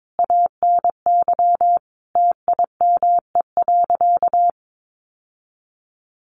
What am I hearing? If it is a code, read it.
Morse code, heard as a keyed tone, about 22 wpm, 700 Hz. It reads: ANY TIME.
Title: Sentences from Top 1100 Words 22wpm